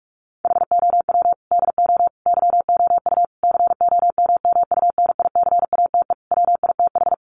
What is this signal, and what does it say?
Morse code, heard as a keyed tone, about 44 wpm, 700 Hz. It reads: HOW DO YOU COMMUNICATE WITH
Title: Sentences from Top 1300 Words 45wpm